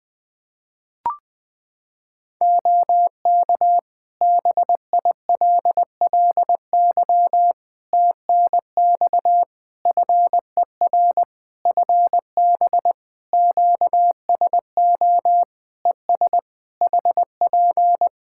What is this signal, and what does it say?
Morse code, heard as a keyed tone, about 20 wpm, 700 Hz. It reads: OK BILLY TNX FER FB QSO ES HP
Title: Practice QSO - Exchange 4 - Ending - 20wpm